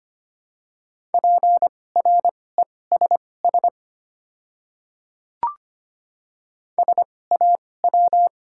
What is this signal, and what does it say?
Morse code, heard as a keyed tone, about 25 wpm, spaced as for 18 wpm, 700 Hz. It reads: PRESS SAW
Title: 2x Character Spacing - Top 500 Words - 25wpm